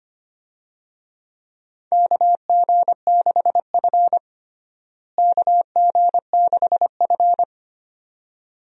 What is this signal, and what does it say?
Morse code, heard as a keyed tone, about 25 wpm, 700 Hz. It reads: KG6F KG6F
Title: USA Call Signs - 4 Characters - Repeated 2x - 25wpm